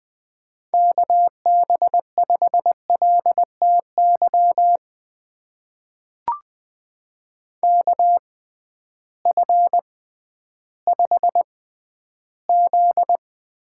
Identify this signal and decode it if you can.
Morse code, heard as a keyed tone, about 20 wpm, spaced as for 12 wpm, 700 Hz. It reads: KB5LTY KF5Z
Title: USA Call Signs - 6 Characters - 6x Character Spacing - 20wpm